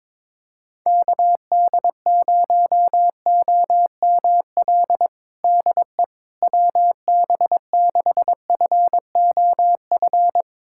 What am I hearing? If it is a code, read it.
Morse code, heard as a keyed tone, about 22 wpm, 700 Hz. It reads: KD0OML DE WB6FOF
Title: USA Callsign DE Callsign 22wpm